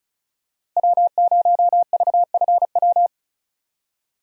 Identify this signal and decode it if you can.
Morse code, heard as a keyed tone, about 36 wpm, 700 Hz. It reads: W0VFW
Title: USA Call Signs - 5 Characters - Repeated 2x - 35wpm